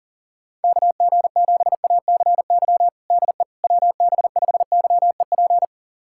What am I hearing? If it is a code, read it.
Morse code, heard as a keyed tone, about 40 wpm, 700 Hz. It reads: KG7ACY DE WB5YEP
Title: USA Callsign DE Callsign - 6 Characters - 40wpm